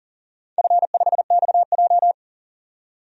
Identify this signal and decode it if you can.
Morse code, heard as a keyed tone, about 40 wpm, 700 Hz. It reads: F5XJ
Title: DXCC-100 Call Signs - 4 Characters - 40wpm